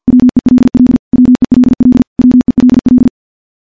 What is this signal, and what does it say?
Morse code, heard as a keyed tone, about 52 wpm, 250 Hz. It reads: WELL WELL WELL
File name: morse / morse-training-50WPM